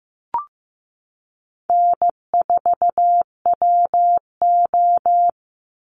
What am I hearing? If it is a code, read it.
Morse code, heard as a keyed tone, about 15 wpm, 700 Hz. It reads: N4WO
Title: Call Signs Repeated 3x - 15wpm